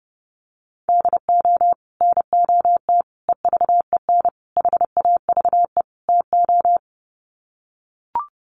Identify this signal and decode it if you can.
Morse code, heard as a keyed tone, about 28 wpm, 700 Hz. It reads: DO NOT EVEN HAVE TO
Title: Sets of 5 Words Encore2 30wpm